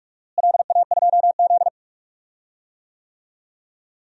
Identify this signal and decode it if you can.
Morse code, heard as a keyed tone, about 41 wpm, 700 Hz. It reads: LA2Z
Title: DXCC-100 Call Signs - 4 Characters - Repeated 3x - 45wpm